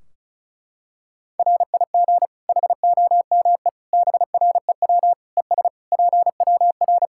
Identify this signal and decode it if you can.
Morse code, heard as a keyed tone, about 34 wpm, 700 Hz. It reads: RIG HOME BREW ES PWR
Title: Practice QSO - Exchange 2 - Info - 35wpm